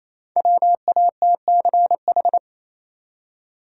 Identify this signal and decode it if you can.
Morse code, heard as a keyed tone, about 27 wpm, 700 Hz. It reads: WATCH
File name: Easy Words - Speed-Racing with Hints - 22wpm